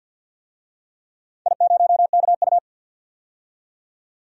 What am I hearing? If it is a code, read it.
Morse code, heard as a keyed tone, about 50 wpm, 700 Hz. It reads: I0KU
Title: DXCC-100 Call Signs - 4 Characters - 50wpm